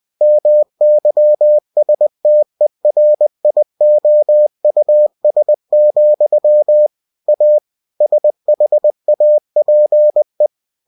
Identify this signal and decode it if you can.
Morse code, heard as a keyed tone, about 20 wpm, 600 Hz. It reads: MYSTERIOUS, A SHAPE
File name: cw-books / winnie-the-pooh / winnie-20wpm-page-0040